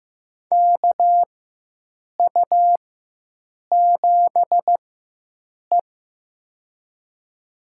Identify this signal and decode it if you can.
Morse code, heard as a keyed tone, about 15 wpm, spaced as for 7 wpm, 700 Hz. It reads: KU7E
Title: USA Call Signs - 4 Characters - 4x Character Spacing - 15wpm